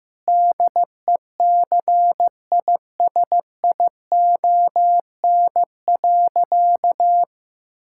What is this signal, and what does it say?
Morse code, heard as a keyed tone, about 16 wpm, 700 Hz. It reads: DECISION.
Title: Sentences from Top 1600 Words 15wpm